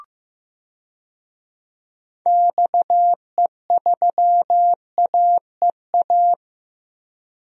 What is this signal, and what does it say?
Morse code, heard as a keyed tone, about 16 wpm, 700 Hz. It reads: XE3AEA
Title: DXCC-100 Call Signs - 6 Characters - Repeated 3x - 15wpm